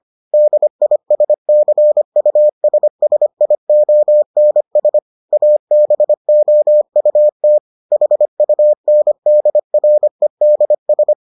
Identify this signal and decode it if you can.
Morse code, heard as a keyed tone, about 25 wpm, 600 Hz. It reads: DISCUSSIONS ABOUT HUNDREDS